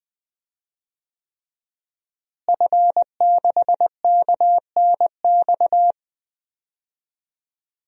Morse code, heard as a keyed tone, about 20 wpm, 700 Hz. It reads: F6KNX
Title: DXCC-100 Call Signs - 5 Characters - Repeated 3x - 20wpm